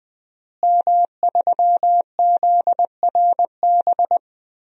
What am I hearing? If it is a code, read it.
Morse code, heard as a keyed tone, about 20 wpm, 700 Hz. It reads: M3ZRB
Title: DXCC-100 Call Signs - 5 Characters - Repeated 3x - 20wpm